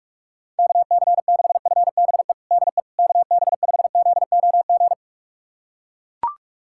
Morse code, heard as a keyed tone, about 41 wpm, 700 Hz. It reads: KC6FBE DE KB5ZOG
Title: USA Callsign DE Callsign - 6 Characters - 45wpm